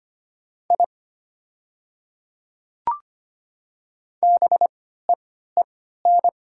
Morse code, heard as a keyed tone, about 25 wpm, spaced as for 14 wpm, 700 Hz. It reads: I BEEN
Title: 3x Character Spacing - Top 100 Words - 25wpm